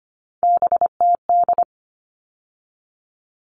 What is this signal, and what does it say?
Morse code, heard as a keyed tone, about 25 wpm, 700 Hz. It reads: BTD
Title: All Cut Numbers - 3-Digit 25wpm